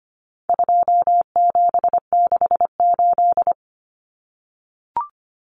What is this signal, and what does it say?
Morse code, heard as a keyed tone, about 25 wpm, 700 Hz. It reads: 2768
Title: Common Cut Numbers - 4-Digit 25wpm